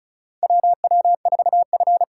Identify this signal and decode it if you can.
Morse code, heard as a keyed tone, about 34 wpm, 700 Hz. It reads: WW4F
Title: USA Call Signs - 4x Character Spacing - 35wpm